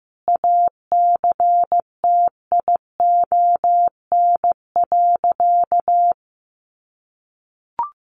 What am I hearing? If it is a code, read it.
Morse code, heard as a keyed tone, about 15 wpm, 700 Hz. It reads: ACTION.
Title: Sentences from Top 700 Words 15wpm